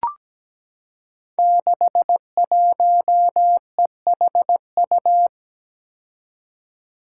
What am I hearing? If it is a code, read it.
Morse code, heard as a keyed tone, about 17 wpm, 700 Hz. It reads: 61EHU
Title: Five Challenging Letter-Number Combinations 17wpm